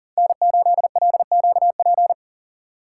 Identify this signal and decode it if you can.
Morse code, heard as a keyed tone, about 40 wpm, 700 Hz. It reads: N8LQP
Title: USA Call Signs - 5 Characters - 4x Character Spacing - 40wpm